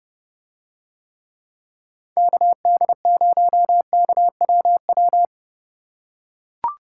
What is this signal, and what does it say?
Morse code, heard as a keyed tone, about 30 wpm, 700 Hz. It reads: KD0KWW